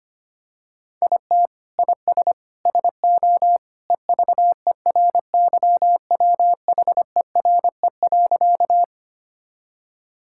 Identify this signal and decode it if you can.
Morse code, heard as a keyed tone, about 25 wpm, 700 Hz. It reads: IT IS SO EVERYWHERE.
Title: Sentences from Top 1600 Words 25wpm